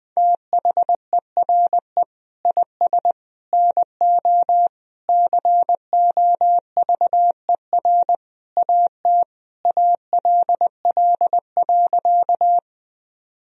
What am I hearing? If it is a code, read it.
Morse code, heard as a keyed tone, about 20 wpm, 700 Hz. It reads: THERE IS NO COVER AT ALL.
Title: Sentences from Top 1000 Words - Review Part 1 - 20wpm